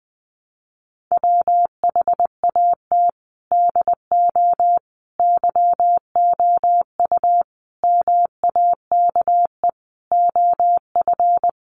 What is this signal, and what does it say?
Morse code, heard as a keyed tone, about 20 wpm, 700 Hz. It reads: WHAT DO YOU MAKE OF
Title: Sentences from Top 100 Words 20wpm